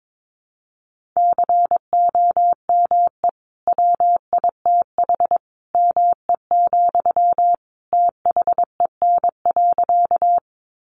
Morse code, heard as a keyed tone, about 22 wpm, 700 Hz. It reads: COME WITH ME, THEN.
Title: Sentences from Top 200 Words 22wpm